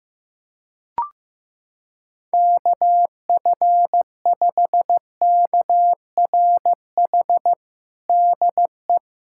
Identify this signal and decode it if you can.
Morse code, heard as a keyed tone, about 15 wpm, 700 Hz. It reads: KF5KRH DE
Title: USA Callsign DE Callsign 15wpm